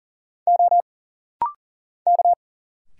Morse code, heard as a keyed tone, about 39 wpm, 700 Hz. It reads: O K
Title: Single Letters - Rapid-Fire 40wpm